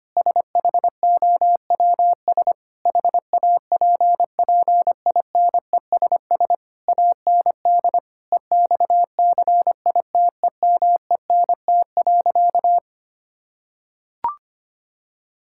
Morse code, heard as a keyed tone, about 25 wpm, 700 Hz. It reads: SHOWS HAPPINESS AND EXCITEMENT.
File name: practice / 25wpm / Difficult Sentences with Hints - 25wpm